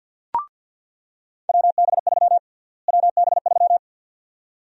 Morse code, heard as a keyed tone, about 50 wpm, 700 Hz. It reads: WB3 WB3
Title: USA Call Sign Prefixes - Repeated 2x - 50wpm